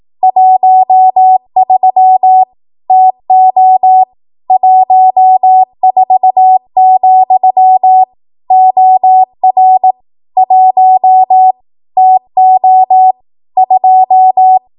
Morse code, heard as a keyed tone, about 18 wpm, 750 Hz. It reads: 13 TO 14, OR 1 TO 2
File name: Morse / 18WPM